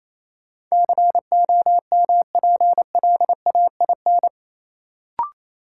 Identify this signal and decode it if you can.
Morse code, heard as a keyed tone, about 28 wpm, 700 Hz. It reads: COMPLAIN
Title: Sentences from Top 1300 Words - New Word Drill - 28wpm